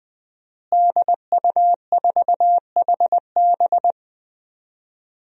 Morse code, heard as a keyed tone, about 20 wpm, 700 Hz. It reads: DU4HB
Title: DXCC-100 Call Signs - 5 Characters - Repeated 3x - 20wpm